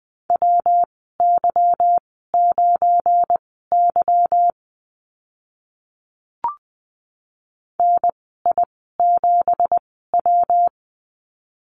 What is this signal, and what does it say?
Morse code, heard as a keyed tone, about 20 wpm, spaced as for 14 wpm, 700 Hz. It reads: WY9Y NI7W
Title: 238 Lesson 38g - Letter q - Call Signs 20wpm